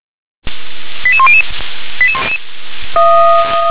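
Five-tone selective call (ZVEI), heard as quick runs of five tones, tones of about 70 ms. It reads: 88100
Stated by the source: ZVEI five-tone selective call, code 88641, received by SDR.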